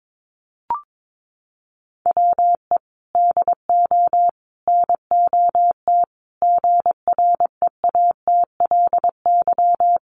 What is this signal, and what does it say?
Morse code, heard as a keyed tone, about 22 wpm, 700 Hz. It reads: WE DO NOT GREATLY
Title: Sentences from Top 1400 Words 22wpm